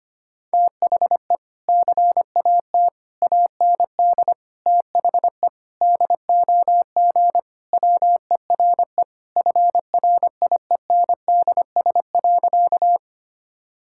Morse code, heard as a keyed tone, about 25 wpm, 700 Hz. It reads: THE CAT AND THE DOG WERE FRIENDS.
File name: practice / 25wpm / Sentences with 2-Word Phrases - 25wpm